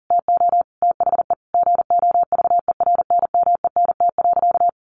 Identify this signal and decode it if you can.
Morse code, heard as a keyed tone, about 37 wpm, 700 Hz. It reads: TO THE GOVERNMENT.
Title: A Sound Of Thunder 40wpm